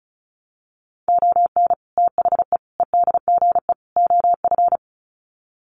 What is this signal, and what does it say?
Morse code, heard as a keyed tone, about 34 wpm, 700 Hz. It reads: ON THE EDGE OF
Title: Sets of 4 Words 35wpm